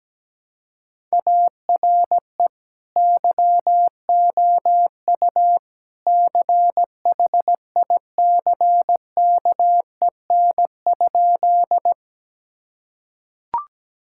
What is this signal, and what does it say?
Morse code, heard as a keyed tone, about 17 wpm, 700 Hz. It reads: ARE YOU CHICKEN?
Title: Sentences from Top 1000 Words - Review Part 5 - 17wpm